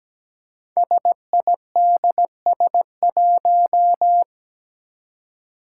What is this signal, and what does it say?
Morse code, heard as a keyed tone, about 17 wpm, 700 Hz. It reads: SIDS1
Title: Five Challenging Letter-Number Combinations 17wpm